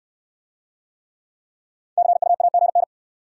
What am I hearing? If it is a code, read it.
Morse code, heard as a keyed tone, about 60 wpm, 700 Hz. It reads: CU AGN